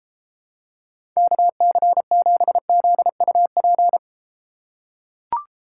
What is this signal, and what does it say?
Morse code, heard as a keyed tone, about 32 wpm, 700 Hz. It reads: KC7ZUP